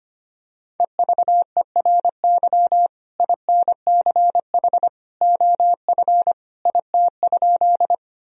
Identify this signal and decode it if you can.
Morse code, heard as a keyed tone, about 25 wpm, 700 Hz. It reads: EVERY INCH OF IT?
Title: Sentences from Top 1000 Words - Review Part 3 - 25wpm